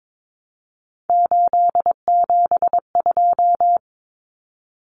Morse code, heard as a keyed tone, about 22 wpm, 700 Hz. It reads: 872
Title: Common Cut Numbers - 3-Digit 22wpm